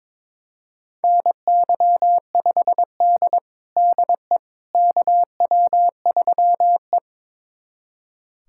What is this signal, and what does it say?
Morse code, heard as a keyed tone, about 22 wpm, 700 Hz. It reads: NY5D DE KW3E
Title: USA Callsign DE Callsign - 4 Characters - 22wpm